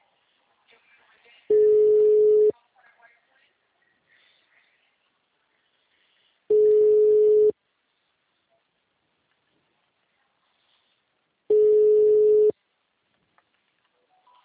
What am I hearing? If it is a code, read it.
Ringback tone (Europe, 425 Hz).